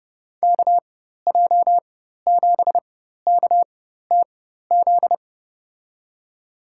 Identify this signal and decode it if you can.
Morse code, heard as a keyed tone, about 31 wpm, spaced as for 14 wpm, 700 Hz. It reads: KJ7KTZ
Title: USA Call Signs - 6 Characters - 4x Character Spacing - 30wpm